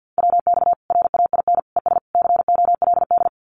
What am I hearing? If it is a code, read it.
Morse code, heard as a keyed tone, about 50 wpm, 700 Hz. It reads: WX RAIN ES COLD
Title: Common QSO Phrases 50wpm